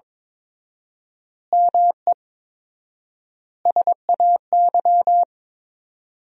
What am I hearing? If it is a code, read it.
Morse code, heard as a keyed tone, about 23 wpm, 700 Hz. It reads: ME SAY
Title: 4x Word-Spacing - Sentences from Top 500 Words - 22wpm